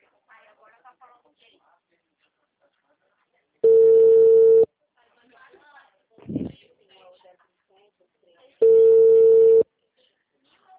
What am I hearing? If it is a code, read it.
Ringback tone (China, 450 Hz), with speech.